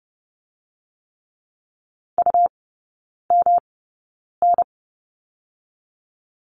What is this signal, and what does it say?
Morse code, heard as a keyed tone, about 30 wpm, spaced as for 9 wpm, 700 Hz. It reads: UMN
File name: Lesson 18e - Letter m - Three Characters 30wpm